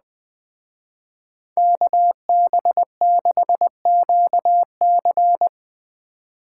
Morse code, heard as a keyed tone, about 21 wpm, 700 Hz. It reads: KB6QC